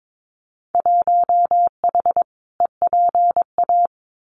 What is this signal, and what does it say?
Morse code, heard as a keyed tone, about 22 wpm, 700 Hz. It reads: 1H EPA
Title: Winter Field Day Exchanges 22wpm